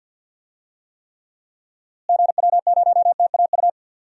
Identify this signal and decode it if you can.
Morse code, heard as a keyed tone, about 52 wpm, 700 Hz. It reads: GW0TAU
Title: DXCC-100 Call Signs - 6 Characters - Repeated 3x - 50wpm